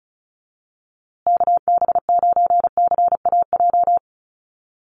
Morse code, heard as a keyed tone, about 34 wpm, 700 Hz. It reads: KB9CAJ
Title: USA Call Signs Repeated 2x 35wpm